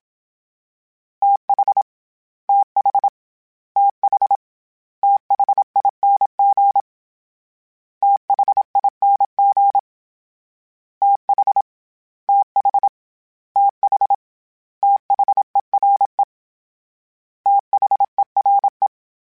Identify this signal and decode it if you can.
Morse code, heard as a keyed tone, about 27 wpm, 800 Hz. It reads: TH TH TH THING THING TH TH TH THERE THERE
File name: THs / TH / TH Prefixes 3 @25